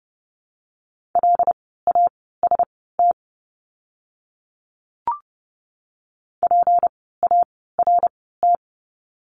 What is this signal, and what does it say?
Morse code, heard as a keyed tone, about 29 wpm, spaced as for 17 wpm, 700 Hz. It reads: LAST PART
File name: course / 30:10wpm / 237 Lesson 38f - Letter q - Words 30wpm